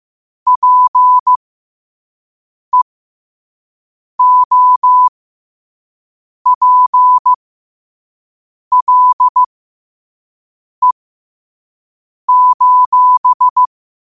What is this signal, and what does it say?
Morse code, heard as a keyed tone, about 15 wpm, spaced as for 5 wpm, 1000 Hz. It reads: PEOPLE:
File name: qotdfiles / 5wpm